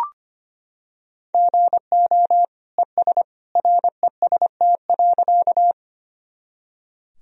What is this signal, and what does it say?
Morse code, heard as a keyed tone, about 25 wpm, 700 Hz. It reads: GO ES REST.
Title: Sentences from Top 500 Words and Abbreviations 25wpm